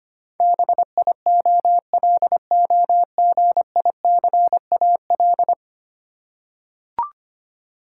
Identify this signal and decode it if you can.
Morse code, heard as a keyed tone, about 25 wpm, 700 Hz. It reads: BIOLOGICAL
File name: Ten Letter Words - 25wpm